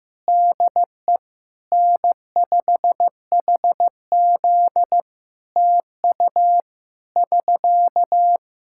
Morse code, heard as a keyed tone, about 15 wpm, 700 Hz. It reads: DE N5HZ TU <SK>
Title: Practice QSO - Exchange 4 - Ending - 15wpm